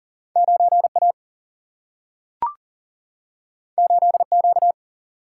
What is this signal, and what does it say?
Morse code, heard as a keyed tone, about 40 wpm, 700 Hz. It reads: 9 A 8 Q
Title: Single-Letter Single-Number 40wpm